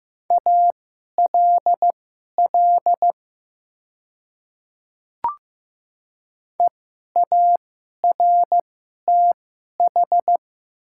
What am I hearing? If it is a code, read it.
Morse code, heard as a keyed tone, about 15 wpm, spaced as for 11 wpm, 700 Hz. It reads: ALL EARTH
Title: 2x Character Spacing - Top 200 Words - 15wpm